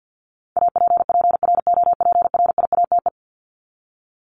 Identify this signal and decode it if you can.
Morse code, heard as a keyed tone, about 48 wpm, 700 Hz. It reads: APPROPRIATE
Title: Top 751-1000 Words 50wpm